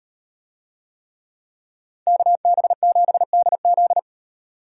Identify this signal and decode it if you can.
Morse code, heard as a keyed tone, about 38 wpm, 700 Hz. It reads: KB7DZ